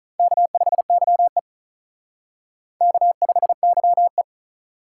Morse code, heard as a keyed tone, about 43 wpm, 700 Hz. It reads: K5YE K5YE
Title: Speed Racing - USA Call Signs 35wpm